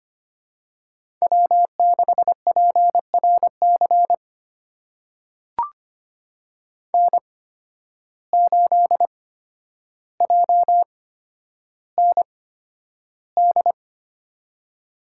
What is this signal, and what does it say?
Morse code, heard as a keyed tone, about 25 wpm, spaced as for 11 wpm, 700 Hz. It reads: W6PRC N8JND
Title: USA Call Signs - 5 Characters - 8x Character Spacing - 25wpm